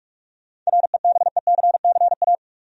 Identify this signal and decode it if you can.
Morse code, heard as a keyed tone, about 46 wpm, 700 Hz. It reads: REBECCA
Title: USA Common Names - 45wpm